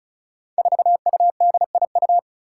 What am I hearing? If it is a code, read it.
Morse code, heard as a keyed tone, about 36 wpm, 700 Hz. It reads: 4UDIU
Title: Five Challenging Letter-Number Combinations 35wpm